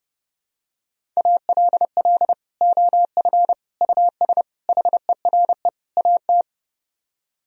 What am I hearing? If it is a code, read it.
Morse code, heard as a keyed tone, about 32 wpm, 700 Hz. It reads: ALL OF US HERE AT